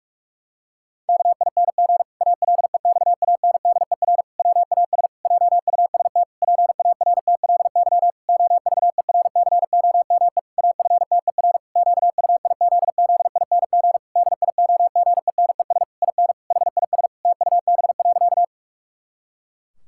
Morse code, heard as a keyed tone, about 41 wpm, 700 Hz. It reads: KING ALEXANDER WAS JUST PARTLY OVERCOME AFTER QUIZZING DIOGENES IN HIS TUB.
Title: Pangram Challenge v2 - 45wpm